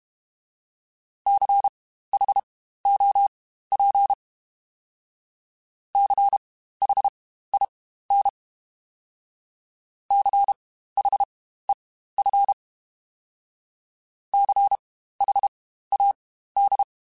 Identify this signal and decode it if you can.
Morse code, heard as a keyed tone, about 32 wpm, spaced as for 15 wpm, 800 Hz. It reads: CHOP CHIN CHEF CHAD
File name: Spaced Alphas / Spaced Alphas mp3s / Cs sp @30